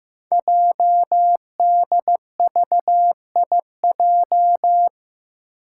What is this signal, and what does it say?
Morse code, heard as a keyed tone, about 15 wpm, 700 Hz. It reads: JDVIJ
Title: Code Groups - 5 Letter:Numbers - 15wpm